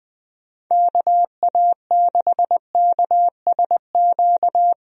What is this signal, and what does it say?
Morse code, heard as a keyed tone, about 20 wpm, 700 Hz. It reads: KA6KSQ
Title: USA Call Signs - 6 Characters - Repeated 3x - 20wpm